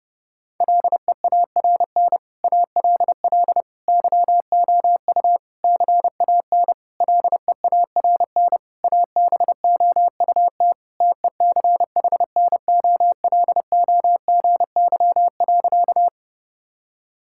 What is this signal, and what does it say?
Morse code, heard as a keyed tone, about 30 wpm, 700 Hz. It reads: LEARN ALL YOU CAN LEARN ABOUT TECHNOLOGY.
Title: Sentences from Top 1000 Words - Review Part 5 - 30wpm